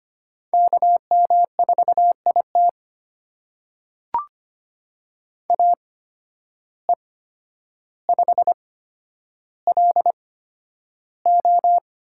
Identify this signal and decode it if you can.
Morse code, heard as a keyed tone, about 25 wpm, spaced as for 11 wpm, 700 Hz. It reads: KM4IT AE5LO
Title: USA Call Signs - 5 Characters - 8x Character Spacing - 25wpm